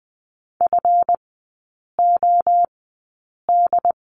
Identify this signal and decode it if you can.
Morse code, heard as a keyed tone, about 20 wpm, spaced as for 8 wpm, 700 Hz. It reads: FOD
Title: 184 Lesson 30e - Letter b - Three Characters 20wpm